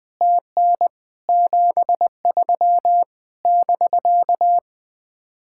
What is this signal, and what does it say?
Morse code, heard as a keyed tone, about 20 wpm, 700 Hz. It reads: TN 73 <BK>
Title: Speed-Racing Common QSO Phrases - 20wpm